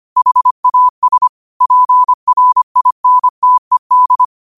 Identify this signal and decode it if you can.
Morse code, heard as a keyed tone, about 25 wpm, 1000 Hz. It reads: HAS PRINTED
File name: qotdfiles / 25wpm